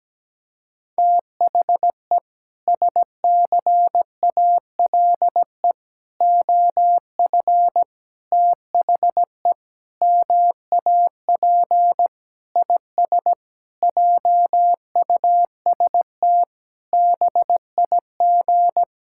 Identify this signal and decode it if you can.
Morse code, heard as a keyed tone, about 17 wpm, 700 Hz. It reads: THE SCALE OF THE MAP IS JUST BIG
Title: Sentences from Top 1000 Words - Review Part 5 - 17wpm